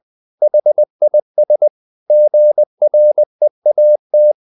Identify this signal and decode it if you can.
Morse code, heard as a keyed tone, about 20 wpm, 600 Hz. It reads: HIS GREAT
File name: the old man and the sea / the old man and the sea part 1 20wpm